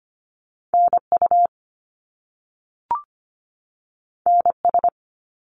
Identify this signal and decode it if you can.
Morse code, heard as a keyed tone, about 25 wpm, 700 Hz. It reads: NU NS
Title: Canadian Provinces and Territories 25wpm